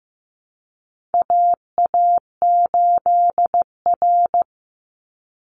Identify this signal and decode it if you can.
Morse code, heard as a keyed tone, about 15 wpm, 700 Hz. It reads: AA8R
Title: Call Signs Repeated 3x - 15wpm